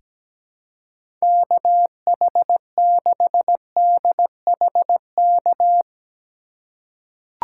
Morse code, heard as a keyed tone, about 17 wpm, 700 Hz. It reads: KH6DHK
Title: K1USN SST QSO Practice - 17wpm